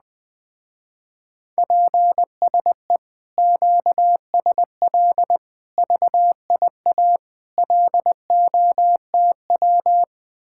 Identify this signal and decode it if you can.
Morse code, heard as a keyed tone, about 20 wpm, 700 Hz. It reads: PSE QSL VIA LOTW
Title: Speed-Racing Common QSO Phrases - 17wpm